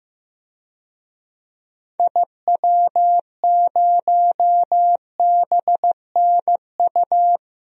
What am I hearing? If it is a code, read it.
Morse code, heard as a keyed tone, about 16 wpm, 700 Hz. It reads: IW0BNU
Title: DXCC-100 Call Signs - 6 Characters - 15wpm